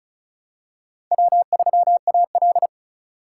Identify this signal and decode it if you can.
Morse code, heard as a keyed tone, about 36 wpm, 700 Hz. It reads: W3AL
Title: USA Call Signs - 4 Characters - 3x Character Spacing - 35wpm